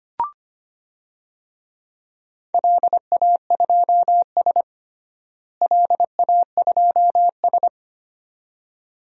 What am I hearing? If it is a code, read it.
Morse code, heard as a keyed tone, about 25 wpm, 700 Hz. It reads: LA2S LA2S
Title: DXCC-100 Call Signs - 4 Characters - Repeated 2x - 25wpm